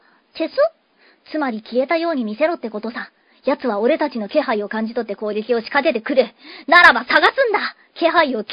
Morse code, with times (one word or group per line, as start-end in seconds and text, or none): none